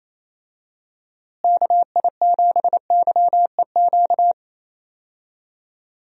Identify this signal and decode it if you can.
Morse code, heard as a keyed tone, about 28 wpm, 700 Hz. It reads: KI7YEQ
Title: USA Call Signs - 6 Characters - Repeated 3x - 28wpm